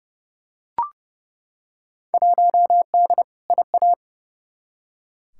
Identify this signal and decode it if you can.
Morse code, heard as a keyed tone, about 30 wpm, 700 Hz. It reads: 1D IA
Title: Field Day Exchanges - 30wpm